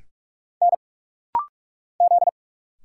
Morse code, heard as a keyed tone, about 44 wpm, 700 Hz. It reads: N Z
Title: Single Letter-Number Alt - Rapid-Fire 45wpm